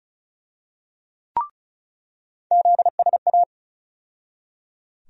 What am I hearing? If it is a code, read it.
Morse code, heard as a keyed tone, about 36 wpm, 700 Hz. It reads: ZSA
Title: USA Call Sign Postfixes - 3 Letters 35wpm